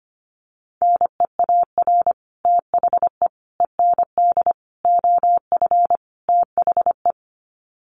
Morse code, heard as a keyed tone, about 25 wpm, 700 Hz. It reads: NEAR THE END OF THE
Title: Sets of 5 Words Encore 25wpm